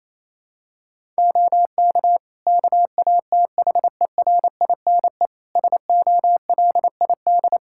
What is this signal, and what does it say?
Morse code, heard as a keyed tone, about 27 wpm, 700 Hz. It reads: OK KATHERINE SOLID
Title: Practice QSO - Exchange 3 - More Info - 28wpm